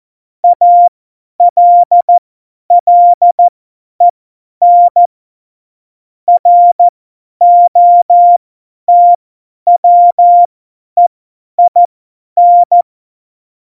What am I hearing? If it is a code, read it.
Morse code, heard as a keyed tone, about 14 wpm, spaced as for 10 wpm, 700 Hz. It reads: ALLEN ROTWEIN
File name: Morsekurs / Witze / Witz 18. Februar 2020, 50 F70